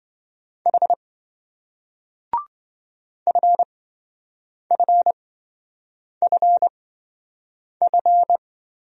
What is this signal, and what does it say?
Morse code, heard as a keyed tone, about 27 wpm, spaced as for 7 wpm, 700 Hz. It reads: H FFFF